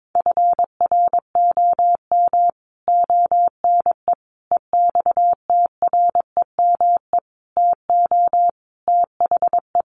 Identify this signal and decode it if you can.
Morse code, heard as a keyed tone, about 22 wpm, 700 Hz. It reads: FROM ONE EXTREME TO THE
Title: Sentences from Top 1300 Words 22wpm